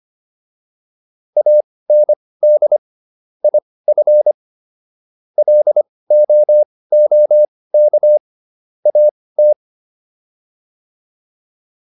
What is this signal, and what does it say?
Morse code, heard as a keyed tone, about 25 wpm, spaced as for 18 wpm, 600 Hz. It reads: AND IF LOOK AT